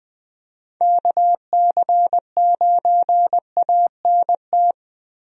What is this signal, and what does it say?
Morse code, heard as a keyed tone, about 20 wpm, 700 Hz. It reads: KC9ANT
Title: USA Call Signs - 6 Characters - 20wpm